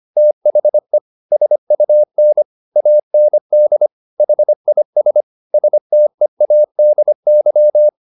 Morse code, heard as a keyed tone, about 25 wpm, 600 Hz. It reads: THE SUN AND HIS STEADY
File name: the old man and the sea part 1 25wpm